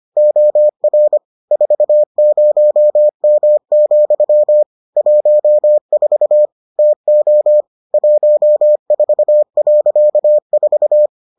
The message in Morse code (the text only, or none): OR 40M, 14 TO 14.4